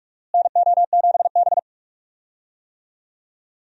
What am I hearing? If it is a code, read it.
Morse code, heard as a keyed tone, about 46 wpm, 700 Hz. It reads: NO7B
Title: USA Call Signs - 4 Characters - Repeated 2x - 45wpm